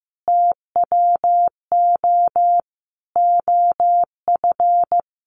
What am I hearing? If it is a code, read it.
Morse code, heard as a keyed tone, about 15 wpm, 700 Hz. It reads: TWO OF
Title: Sentences from Top 100 Words 15wpm